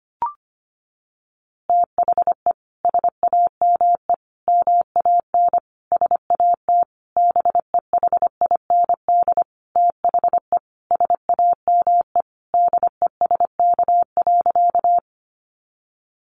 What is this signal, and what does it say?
Morse code, heard as a keyed tone, about 25 wpm, 700 Hz. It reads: THE SAME MAN SAT BEHIND THE SAME DESK.
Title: A Sound Of Thunder 25wpm